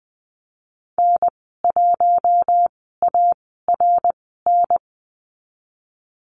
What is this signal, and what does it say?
Morse code, heard as a keyed tone, about 20 wpm, spaced as for 15 wpm, 700 Hz. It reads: N1ARN
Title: 082 Lesson 14g - Number 2 - Call Signs 20wpm